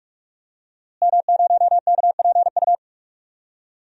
Morse code, heard as a keyed tone, about 48 wpm, 700 Hz. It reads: M0KPU
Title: DXCC-100 Call Signs - 5 Characters - Repeated 3x - 45wpm